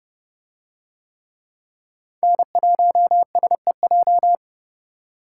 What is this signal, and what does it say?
Morse code, heard as a keyed tone, about 29 wpm, 700 Hz. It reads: N1SEJ